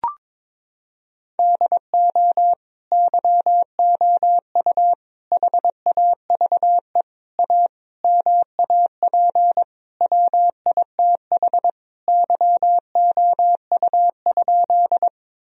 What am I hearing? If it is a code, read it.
Morse code, heard as a keyed tone, about 22 wpm, 700 Hz. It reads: DO YOU HAVE A MAP WITH YOU?
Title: Sentences from Top 1000 Words - Review Part 2 - 22wpm